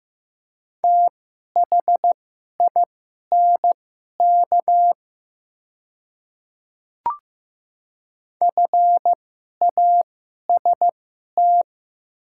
Morse code, heard as a keyed tone, about 15 wpm, spaced as for 11 wpm, 700 Hz. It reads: THINK FAST
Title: 2x Character Spacing - Top 500 Words - 15wpm